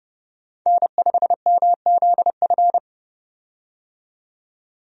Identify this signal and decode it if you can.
Morse code, heard as a keyed tone, about 30 wpm, 700 Hz. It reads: N5MZF
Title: USA Call Signs - 5 Characters - 30wpm